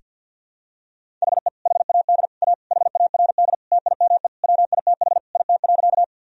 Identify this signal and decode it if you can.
Morse code, heard as a keyed tone, about 51 wpm, 700 Hz. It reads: HE HAD A HARD TIME WITH IT.
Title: Sentences from Top 1000 Words - Review Part 1 - 50wpm